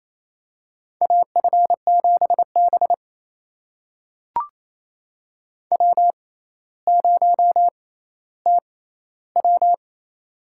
Morse code, heard as a keyed tone, about 28 wpm, spaced as for 14 wpm, 700 Hz. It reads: AF7B W0TW
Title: USA Call Signs - 6x Character Spacing - 28wpm